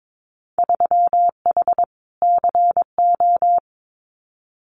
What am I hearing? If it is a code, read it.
Morse code, heard as a keyed tone, about 22 wpm, 700 Hz. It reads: 3H CO
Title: Winter Field Day Exchanges 22wpm